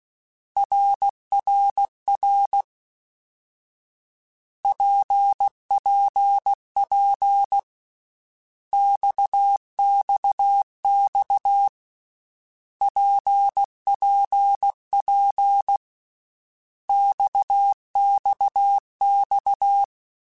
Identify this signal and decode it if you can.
Morse code, heard as a keyed tone, about 16 wpm, 800 Hz. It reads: RRR PPP XXX PPP XXX
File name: Basic Letter Groups By7Groups / RKPX / RKPX@13